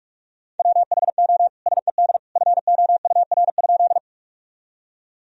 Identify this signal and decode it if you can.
Morse code, heard as a keyed tone, about 41 wpm, 700 Hz. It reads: WHO SED FOUR?
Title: Sentences from Top 300 Words and Abbreviations 45wpm